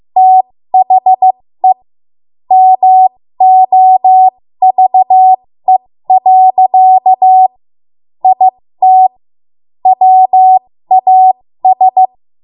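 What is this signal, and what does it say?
Morse code, heard as a keyed tone, about 15 wpm, spaced as for 13 wpm, 750 Hz. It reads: THE MOVE. IT WAS
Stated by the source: ARRL Morse code practice at 13 WPM.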